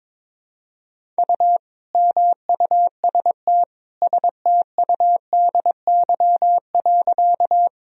Morse code, heard as a keyed tone, about 22 wpm, 700 Hz. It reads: U MUST STUDY.
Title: Sentences from Top 300 Words and Abbreviations 22wpm